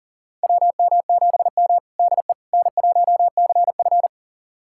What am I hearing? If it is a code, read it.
Morse code, heard as a keyed tone, about 40 wpm, 700 Hz. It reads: WM7M DE N1CF
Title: USA Callsign DE Callsign - 4 Characters - 40wpm